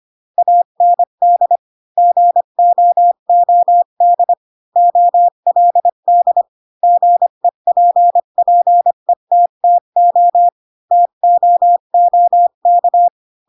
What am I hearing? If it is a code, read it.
Morse code, heard as a keyed tone, about 25 wpm, 700 Hz. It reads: AND GOOD OLD GEPPETTO TOOK
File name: Pinocchio / Pinocchio ch7 20 25 ik1hge